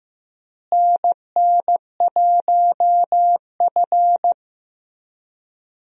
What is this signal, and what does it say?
Morse code, heard as a keyed tone, about 15 wpm, 700 Hz. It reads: NN1F
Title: USA Call Signs - 4 Characters - Repeated 2x - 15wpm